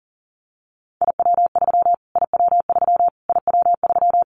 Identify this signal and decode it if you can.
Morse code, heard as a keyed tone, about 40 wpm, 700 Hz. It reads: IW3 IW3 IW3
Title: Callsign Prefixes for Top 25 DXCC - Repeated 3x - 40wpm